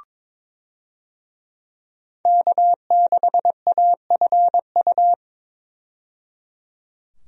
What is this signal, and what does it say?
Morse code, heard as a keyed tone, about 22 wpm, 700 Hz. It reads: K6AFU
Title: USA Call Signs - 5 Characters - 22wpm